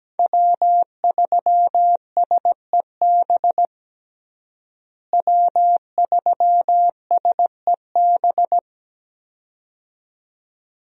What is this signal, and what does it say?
Morse code, heard as a keyed tone, about 17 wpm, 700 Hz. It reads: W3SEB W3SEB
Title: USA Call Signs - 5 Characters - Repeated 2x - 17wpm